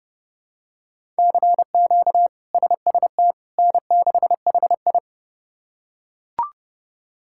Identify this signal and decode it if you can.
Morse code, heard as a keyed tone, about 32 wpm, 700 Hz. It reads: CQ SST N6HI
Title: K1USN SST QSO Practice - 30wpm